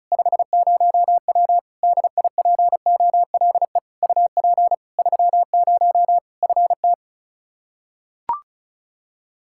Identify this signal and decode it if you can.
Morse code, heard as a keyed tone, about 36 wpm, 700 Hz. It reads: 50W DIPOLE UP 30 FT
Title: Practice QSO - Exchange 2 - Info - 35wpm